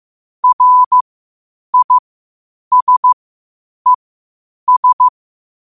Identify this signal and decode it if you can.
Morse code, heard as a keyed tone, about 15 wpm, spaced as for 8 wpm, 1000 Hz. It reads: RISES
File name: qotdfiles / 8wpm